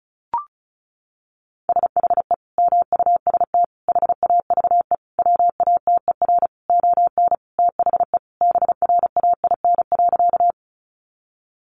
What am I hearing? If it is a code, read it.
Morse code, heard as a keyed tone, about 34 wpm, 700 Hz. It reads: SHE MUST HAVE WATER ON THE BRAIN.
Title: Sentences from Top 1100 Words 35wpm